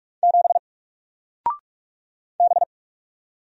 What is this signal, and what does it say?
Morse code, heard as a keyed tone, about 44 wpm, 700 Hz. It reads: 7 B
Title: Single Letter-Number Alt 45wpm